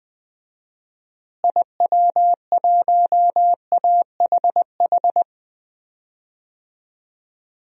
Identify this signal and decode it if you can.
Morse code, heard as a keyed tone, about 20 wpm, 700 Hz. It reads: IW1AHH
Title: DXCC-100 Call Signs - 6 Characters - Repeated 2x - 20wpm